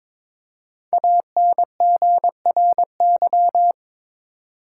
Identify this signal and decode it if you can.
Morse code, heard as a keyed tone, about 23 wpm, 700 Hz. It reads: ANGRY
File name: 5 Letter Words - First 3 Letters Given - 22wpm